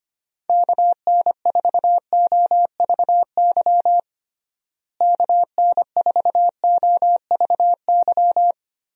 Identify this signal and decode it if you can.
Morse code, heard as a keyed tone, about 25 wpm, 700 Hz. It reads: KN4OVY KN4OVY
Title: USA Call Signs - 6 Characters - Repeated 3x - 25wpm